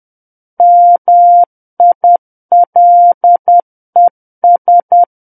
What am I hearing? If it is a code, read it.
Morse code, heard as a keyed tone, about 10 wpm, 700 Hz. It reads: MILES
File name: Book One / 01-Chapter 1 - 10 WPM